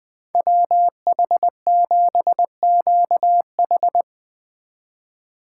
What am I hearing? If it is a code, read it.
Morse code, heard as a keyed tone, about 22 wpm, 700 Hz. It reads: WH7QH
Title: Speed Racing - USA 2x2 Call Signs 20wpm